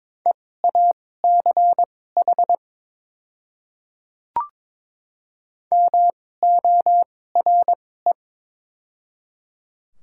Morse code, heard as a keyed tone, about 22 wpm, spaced as for 16 wpm, 700 Hz. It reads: EACH MORE
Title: 2x Character Spacing - Top 100 Words - 22wpm